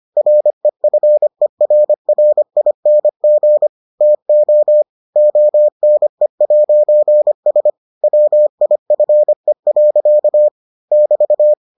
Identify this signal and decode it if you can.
Morse code, heard as a keyed tone, about 25 wpm, 600 Hz. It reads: REFERRING TO ONE'S WIFE. <BT>